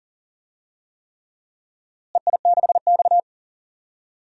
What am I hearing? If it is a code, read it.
Morse code, heard as a keyed tone, about 39 wpm, 700 Hz. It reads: EI6X
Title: DXCC-100 Call Signs - 4 Characters - Repeated 3x - 40wpm